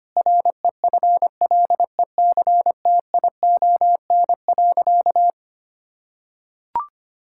Morse code, heard as a keyed tone, about 25 wpm, 700 Hz. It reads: REFLECTION.
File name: practice / 25wpm / Difficult Sentences with Hints - 25wpm